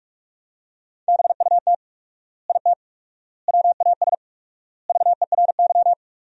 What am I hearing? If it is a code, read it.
Morse code, heard as a keyed tone, about 42 wpm, 700 Hz. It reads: BUT IT WAS VERY
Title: 4x Word-Spacing - Sentences from Top 300 Words - 45wpm